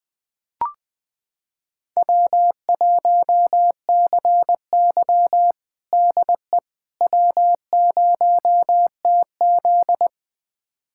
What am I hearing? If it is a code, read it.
Morse code, heard as a keyed tone, about 21 wpm, 700 Hz. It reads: W1CY DE W0TZ
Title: USA Callsign DE Callsign - 4 Characters - 20wpm